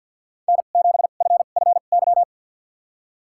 Morse code, heard as a keyed tone, about 52 wpm, spaced as for 37 wpm, 700 Hz. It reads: N7FFY
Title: USA Call Signs - 2x Character Spacing - 50wpm